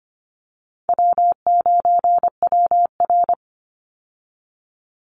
Morse code, heard as a keyed tone, about 25 wpm, 700 Hz. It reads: W9WR
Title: Easy Call Signs Repeated 2x - 25wpm